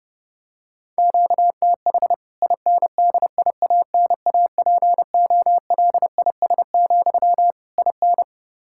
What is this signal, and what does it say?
Morse code, heard as a keyed tone, about 30 wpm, 700 Hz. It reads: QTH INDIANAPOLIS, IN
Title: K1USN SST QSO Practice 30wpm